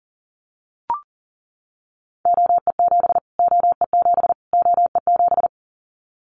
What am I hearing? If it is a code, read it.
Morse code, heard as a keyed tone, about 37 wpm, 700 Hz. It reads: OE7 OE7 OE7
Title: Callsign Prefixes for Top 25 DXCC - Repeated 3x - 40wpm